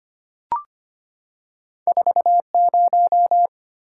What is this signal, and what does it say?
Morse code, heard as a keyed tone, about 25 wpm, 700 Hz. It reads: 40
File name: Two Letter-Number Combinations - 25wpm